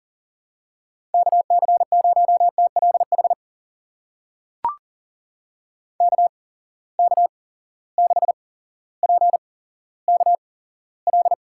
Morse code, heard as a keyed tone, about 38 wpm, spaced as for 17 wpm, 700 Hz. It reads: KC0TLH KK6PKL
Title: USA Call Signs - 6 Characters - 8x Character Spacing - 40wpm